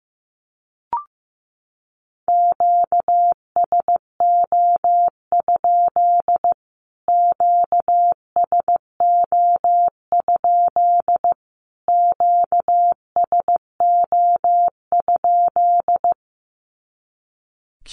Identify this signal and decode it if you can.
Morse code, heard as a keyed tone, about 15 wpm, 700 Hz. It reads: QSO? QSO? QSO?
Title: Q-Codes Repeated 3x 15wpm